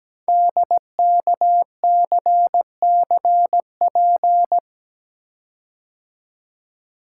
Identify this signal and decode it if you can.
Morse code, heard as a keyed tone, about 17 wpm, 700 Hz. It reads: DKCCP
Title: Code Groups - 5 Letters - 17wpm